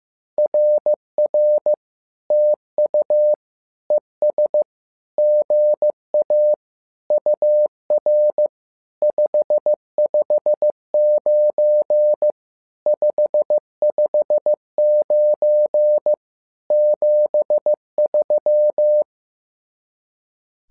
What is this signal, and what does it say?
Morse code, heard as a keyed tone, about 15 wpm, 600 Hz. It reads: RR TU ES GA UR 559 559 73
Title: SOTA QSO Practice 15wpm